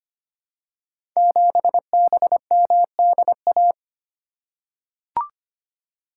Morse code, heard as a keyed tone, about 25 wpm, 700 Hz. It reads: 7BMDA
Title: Code Groups - 5 Letter:Numbers - 25wpm